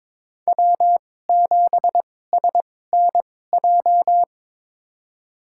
Morse code, heard as a keyed tone, about 22 wpm, spaced as for 16 wpm, 700 Hz. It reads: W7SNJ
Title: USA Call Signs - 5 Characters - 2x Character Spacing - 22wpm